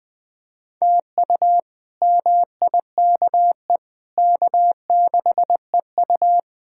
Morse code, heard as a keyed tone, about 20 wpm, 700 Hz. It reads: TU MIKE K6EU
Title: K1USN SST QSO Practice - 20wpm